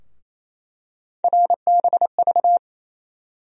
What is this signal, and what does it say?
Morse code, heard as a keyed tone, about 28 wpm, 700 Hz. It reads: RBV
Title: USA Call Sign Postfixes - 3 Letters 28wpm